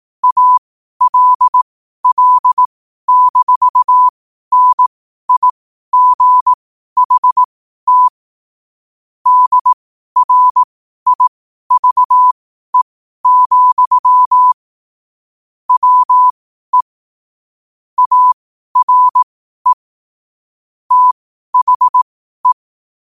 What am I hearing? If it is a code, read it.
Morse code, heard as a keyed tone, about 18 wpm, spaced as for 12 wpm, 1000 Hz. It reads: ALL-NIGHT DRIVE, WE ARE THE
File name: qotdfiles / 12wpm